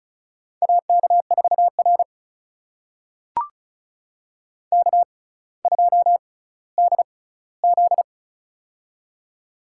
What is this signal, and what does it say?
Morse code, heard as a keyed tone, about 36 wpm, spaced as for 18 wpm, 700 Hz. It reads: AK4R K2DZ
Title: USA Call Signs - 4 Characters - 6x Character Spacing - 35wpm